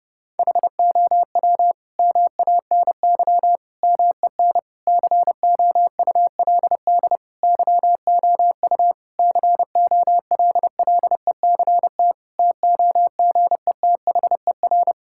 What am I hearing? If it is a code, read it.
Morse code, heard as a keyed tone, about 30 wpm, 700 Hz. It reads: HOW MANY MEN COULD YOU COLLECT TOGETHER
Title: Sentences from Top 1600 Words 30wpm